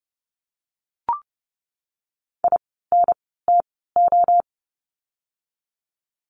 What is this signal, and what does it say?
Morse code, heard as a keyed tone, about 30 wpm, spaced as for 17 wpm, 700 Hz. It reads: INTO
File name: course / 30:10wpm / 237 Lesson 38f - Letter q - Words 30wpm